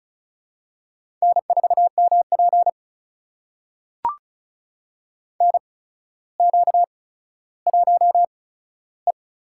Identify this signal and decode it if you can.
Morse code, heard as a keyed tone, about 36 wpm, spaced as for 15 wpm, 700 Hz. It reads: N4MP NQ1E
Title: USA Call Signs - 4 Characters - 8x Character Spacing - 35wpm